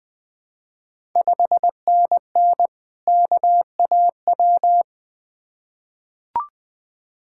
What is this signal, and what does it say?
Morse code, heard as a keyed tone, about 20 wpm, 700 Hz. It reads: 5NN KAW
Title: Ontario QSO Party Exchange 20wpm